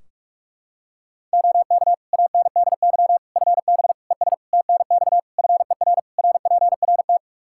Morse code, heard as a keyed tone, about 46 wpm, 700 Hz. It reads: OK ANDY FB ES TNX FER RPRT
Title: K1USN SST QSO Practice 45wpm